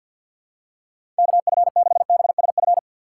Morse code, heard as a keyed tone, about 52 wpm, 700 Hz. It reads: KF6BSF
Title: USA Call Signs - 3x Character Spacing - 50wpm